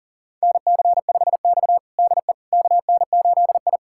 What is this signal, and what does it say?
Morse code, heard as a keyed tone, about 37 wpm, 690 Hz. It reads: NC5X DE KN8I
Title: USA Callsign DE Callsign - 4 Characters - 40wpm